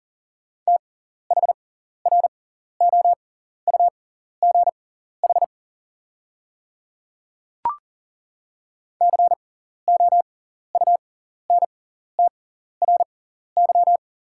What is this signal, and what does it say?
Morse code, heard as a keyed tone, about 40 wpm, spaced as for 14 wpm, 700 Hz. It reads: THROUGH COUNTRY
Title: 6x Character Spacing - Top 500 Words - 40wpm